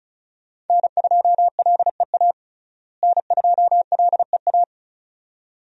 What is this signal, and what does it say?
Morse code, heard as a keyed tone, about 36 wpm, 700 Hz. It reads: N2LEA N2LEA
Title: USA Call Signs - 5 Characters - Repeated 2x - 35wpm